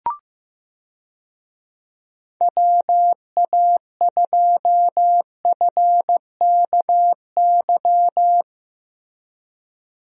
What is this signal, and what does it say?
Morse code, heard as a keyed tone, about 15 wpm, 700 Hz. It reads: WA2FKY
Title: USA Call Signs - 6 Characters - Repeated 3x - 15wpm